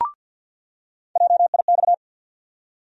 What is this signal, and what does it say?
Morse code, heard as a keyed tone, about 47 wpm, 700 Hz. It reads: JIX
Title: USA Call Sign Postfixes - 3 Letters 50wpm